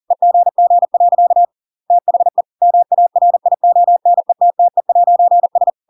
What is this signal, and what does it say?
Morse code, heard as a keyed tone, about 40 wpm, 700 Hz. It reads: EGG. THE MARIONETTE'S
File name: Pinocchio / Pinocchio ch5 40 40 ik1hge